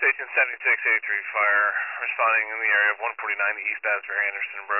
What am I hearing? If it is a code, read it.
Emergency Alert System (SAME) tones, heard as harsh buzzing bursts, with speech. It is heard from 1.0 to 4.5 s.